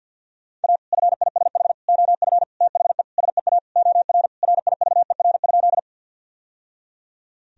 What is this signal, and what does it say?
Morse code, heard as a keyed tone, about 51 wpm, 700 Hz. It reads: A FISH OF THE SEA OR RIVER?
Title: Sentences from Top 1000 Words - Review Part 1 - 50wpm